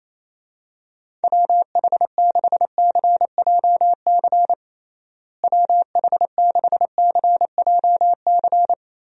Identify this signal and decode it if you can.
Morse code, heard as a keyed tone, about 28 wpm, 700 Hz. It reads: WH6CJC WH6CJC
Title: USA Call Signs - 6 Characters - Repeated 3x - 28wpm